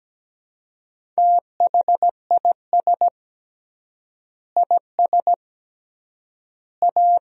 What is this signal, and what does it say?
Morse code, heard as a keyed tone, about 17 wpm, 700 Hz. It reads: THIS IS A
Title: 3x Word-Spacing - Sentences from Top 500 Words - 17wpm